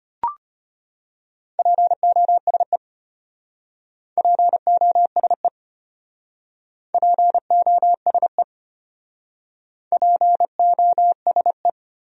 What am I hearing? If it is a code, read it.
Morse code, heard as a keyed tone, about 31 wpm, 700 Hz. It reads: POSE POSE POSE POSE
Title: Speed-Racing Top 500 Words 25wpm